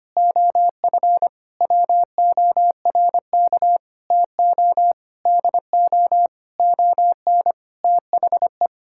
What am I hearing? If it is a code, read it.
Morse code, heard as a keyed tone, about 25 wpm, 700 Hz. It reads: OF WORK TO DO ON THE
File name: practice / 25wpm / Sentences with 9-Word Phrases - 25wpm